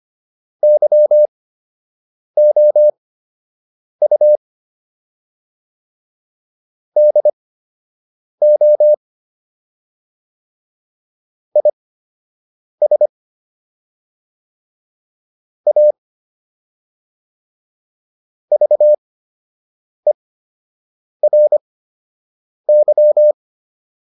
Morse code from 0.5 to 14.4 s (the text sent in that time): YOU DO IS